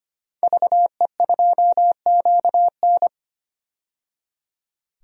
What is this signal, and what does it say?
Morse code, heard as a keyed tone, about 25 wpm, 700 Hz. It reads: VE2QN
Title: DXCC-100 Call Signs - 5 Characters - 25wpm